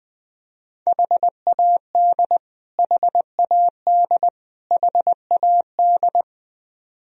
Morse code, heard as a keyed tone, about 21 wpm, 700 Hz. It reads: HAD HAD HAD
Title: Top 100 Words Repeated 20wpm